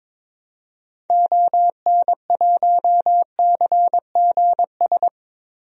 Morse code, heard as a keyed tone, about 22 wpm, 700 Hz. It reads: ON1CGS
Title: DXCC-100 Call Signs - 6 Characters - 22wpm